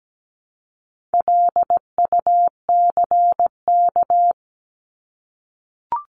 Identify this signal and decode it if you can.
Morse code, heard as a keyed tone, about 17 wpm, 700 Hz. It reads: LUCK
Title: Top 1251-1500 Words 17wpm